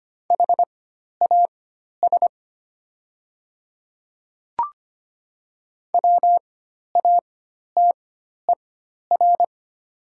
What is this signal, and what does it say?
Morse code, heard as a keyed tone, about 25 wpm, spaced as for 12 wpm, 700 Hz. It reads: HAS WATER
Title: 4x Character Spacing - Top 100 Words - 25wpm